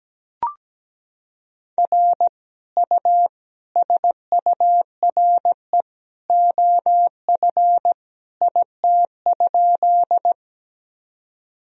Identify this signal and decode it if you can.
Morse code, heard as a keyed tone, about 17 wpm, 700 Hz. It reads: R U SURE OF IT?
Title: Sentences from Top 300 Words and Abbreviations 17wpm